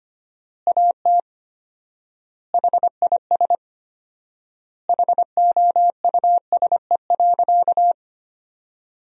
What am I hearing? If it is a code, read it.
Morse code, heard as a keyed tone, about 25 wpm, 700 Hz. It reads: AT HIS HOUSE.
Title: 4x Word-Spacing - Sentences from Top 300 Words - 25wpm